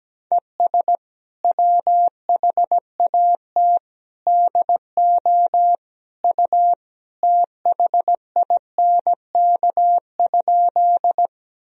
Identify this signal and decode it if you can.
Morse code, heard as a keyed tone, about 17 wpm, 700 Hz. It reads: ES WHAT DO U THINK?
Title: Sentences from Top 200 Words and Abbreviations 17wpm